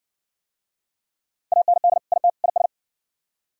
Morse code, heard as a keyed tone, about 69 wpm, 700 Hz. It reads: AND IT IS